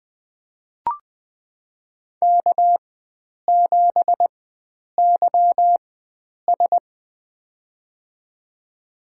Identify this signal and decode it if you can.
Morse code, heard as a keyed tone, about 20 wpm, spaced as for 9 wpm, 700 Hz. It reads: K7YS
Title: USA Call Signs - 4 Characters - 4x Character Spacing - 20wpm